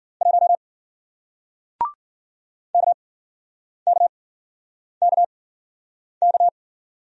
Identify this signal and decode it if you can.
Morse code, heard as a keyed tone, about 55 wpm, 700 Hz. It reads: 1 K K K K